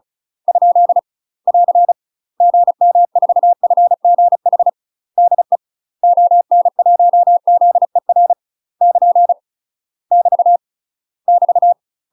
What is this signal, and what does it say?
Morse code, heard as a keyed tone, about 34 wpm, 700 Hz. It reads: ? <AR> GM4FGH DE ON1ZER <KN> <BT> <BT>
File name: UK MASTER QSOs TRAINING / 35WPM / UK MASTER QSOs 0025